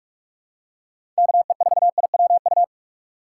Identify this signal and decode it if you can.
Morse code, heard as a keyed tone, about 47 wpm, 700 Hz. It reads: KE4IWU
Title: USA Call Signs - 4x Character Spacing - 45wpm